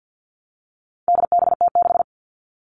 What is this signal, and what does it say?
Morse code, heard as a keyed tone, about 50 wpm, 700 Hz. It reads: DBT6
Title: All Cut Numbers - 4-Digit 50wpm